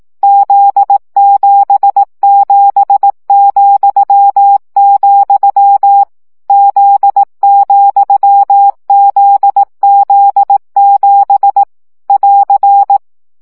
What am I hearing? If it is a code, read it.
Morse code, heard as a keyed tone, about 18 wpm, 800 Hz. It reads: Z77,, Z,ZZ7 <AR>
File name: CW gyakorlas z 7 vesszo 15 baud